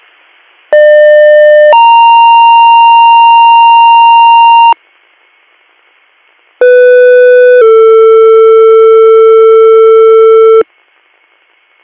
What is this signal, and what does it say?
Two-tone page, heard as a one-second tone, then a three-second tone, about 600 then 930 Hz, about 500 then 450 Hz. It is heard from 0.7 to 10.6 s.